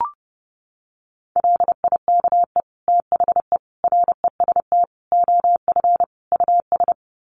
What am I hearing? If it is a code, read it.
Morse code, heard as a keyed tone, about 30 wpm, 700 Hz. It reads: LIKE THE REST OF US
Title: Sets of 5 Words Encore2 30wpm